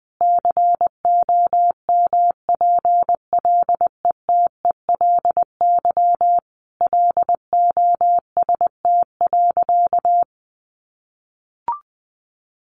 Morse code, heard as a keyed tone, about 20 wpm, 700 Hz. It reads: COMPLETELY LOST.
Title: Additional Easy Phrases 20wpm